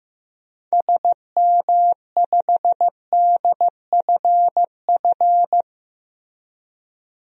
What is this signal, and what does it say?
Morse code, heard as a keyed tone, about 16 wpm, 700 Hz. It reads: SM5DFF
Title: DXCC-100 Call Signs - 6 Characters - Repeated 2x - 15wpm